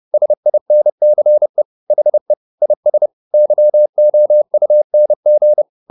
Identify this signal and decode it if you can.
Morse code, heard as a keyed tone, about 30 wpm, 600 Hz. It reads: SINCE HE IS YOUNG
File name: the old man and the sea / the old man and the sea 1 30wpm